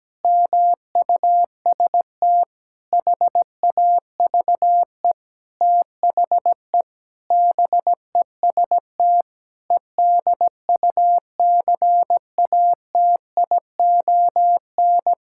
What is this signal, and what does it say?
Morse code, heard as a keyed tone, about 17 wpm, 700 Hz. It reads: MUST HAVE THE BEST EDUCATION
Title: Sentences from Top 1000 Words - Review Part 6 - 17wpm